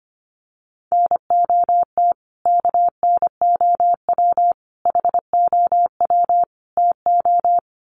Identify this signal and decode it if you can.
Morse code, heard as a keyed tone, about 25 wpm, 700 Hz. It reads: NOT KNOW HOW TO
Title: Sets of 4 Words 25wpm